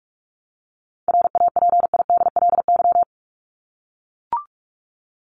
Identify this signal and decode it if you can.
Morse code, heard as a keyed tone, about 44 wpm, 700 Hz. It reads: RAPIDLY
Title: LY Suffix - Top 100 Words with Base Words 45wpm